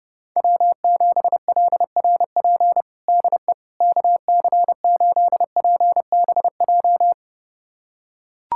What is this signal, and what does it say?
Morse code, heard as a keyed tone, about 32 wpm, 700 Hz. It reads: W7LRP DE KC8PBJ
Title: USA Callsign DE Callsign 30wpm